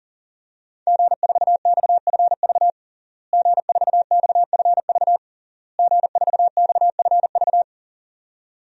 Morse code, heard as a keyed tone, about 40 wpm, 700 Hz. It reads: G4XFV G4XFV G4XFV
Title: DXCC-100 Call Signs - 5 Characters - Repeated 3x - 40wpm